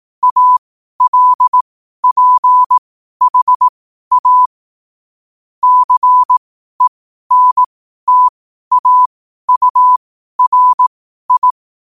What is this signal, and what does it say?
Morse code, heard as a keyed tone, about 18 wpm, spaced as for 12 wpm, 1000 Hz. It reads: ALPHA CENTAURI
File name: qotdfiles / 12wpm